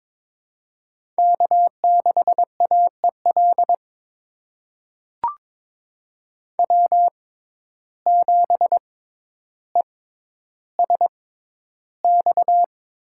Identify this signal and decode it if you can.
Morse code, heard as a keyed tone, about 21 wpm, spaced as for 11 wpm, 700 Hz. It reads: K6AEL W7ESX
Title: USA Call Signs - 5 Characters - 6x Character Spacing - 22wpm